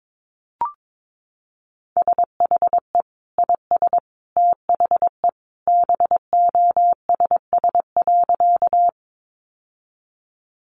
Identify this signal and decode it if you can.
Morse code, heard as a keyed tone, about 22 wpm, 700 Hz. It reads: SHE IS THE BOSS.
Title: Sentences from Top 700 Words 22wpm